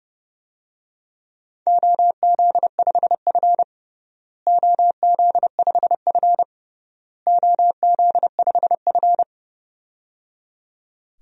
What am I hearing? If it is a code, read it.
Morse code, heard as a keyed tone, about 30 wpm, 700 Hz. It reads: OZ5F OZ5F OZ5F
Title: DXCC-100 Call Signs - 4 Characters - Repeated 3x - 30wpm